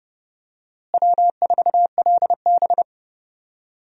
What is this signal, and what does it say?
Morse code, heard as a keyed tone, about 30 wpm, 700 Hz. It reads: W4LB
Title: USA Call Signs - 4 Characters - Repeated 2x - 30wpm